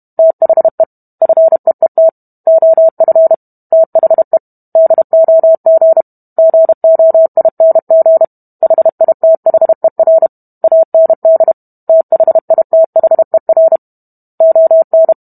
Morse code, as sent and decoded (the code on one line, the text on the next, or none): - .... / ..-. . - / --- ..-. / - .... / -.. --- --. / --. --- .. -. --. / .... .. - .... .-. / .- -. -.. / - .... .. - .... .-. / --- -.
THE FEET OF THE DOG GOING HITHER AND THITHER ON